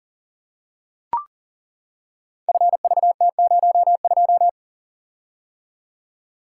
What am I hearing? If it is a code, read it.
Morse code, heard as a keyed tone, about 39 wpm, 700 Hz. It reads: FVT02
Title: Code Groups - 5 Letter:Numbers - 40wpm